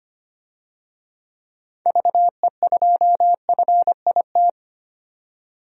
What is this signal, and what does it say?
Morse code, heard as a keyed tone, about 25 wpm, 700 Hz. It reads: VE2FIT
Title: DXCC-100 Call Signs - 6 Characters - 25wpm